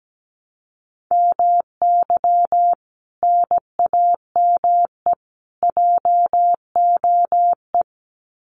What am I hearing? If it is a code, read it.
Morse code, heard as a keyed tone, about 17 wpm, 700 Hz. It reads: MY NAME JOE
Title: Common QSO Phrases 17wpm